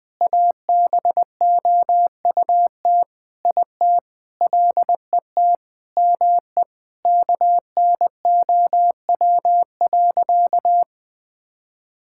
Morse code, heard as a keyed tone, about 21 wpm, 700 Hz. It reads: ABOUT IT LET ME KNOW.
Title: Sentences with 8-Word Phrases - 20wpm